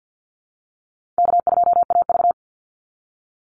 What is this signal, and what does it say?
Morse code, heard as a keyed tone, about 50 wpm, 700 Hz. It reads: K2AV
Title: Easy Call Signs Repeated 2x - 50wpm